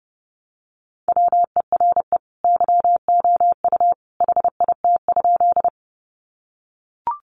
Morse code, heard as a keyed tone, about 30 wpm, 700 Hz. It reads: WERE YOU HIT?
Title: Sentences from Top 1200 Words 30wpm